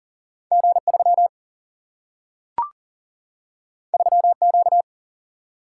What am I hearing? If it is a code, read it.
Morse code, heard as a keyed tone, about 40 wpm, 700 Hz. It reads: G 3 3 Q
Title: Single-Letter Single-Number 40wpm